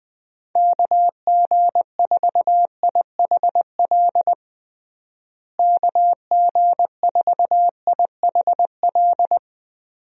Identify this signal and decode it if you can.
Morse code, heard as a keyed tone, about 20 wpm, 700 Hz. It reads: KG4IHL KG4IHL
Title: USA Call Signs - 6 Characters - Repeated 2x - 20wpm